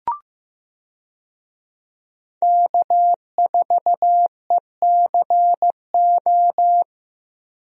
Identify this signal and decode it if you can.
Morse code, heard as a keyed tone, about 16 wpm, 700 Hz. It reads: K4ECO
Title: USA Call Signs - 5 Characters - Repeated 3x - 15wpm